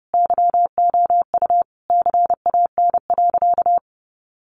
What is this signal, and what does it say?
Morse code, heard as a keyed tone, about 30 wpm, 700 Hz. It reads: YOU CAN.
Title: Sentences from Top 600 Words 30wpm